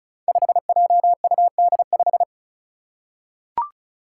Morse code, heard as a keyed tone, about 34 wpm, 700 Hz. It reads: 5JUD5
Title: Five Challenging Letter-Number Combinations 35wpm